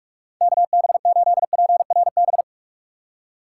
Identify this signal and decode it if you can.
Morse code, heard as a keyed tone, about 60 wpm, spaced as for 52 wpm, 700 Hz. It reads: KB8PRB
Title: Speed Racing - USA 2x3 Call Signs 45wpm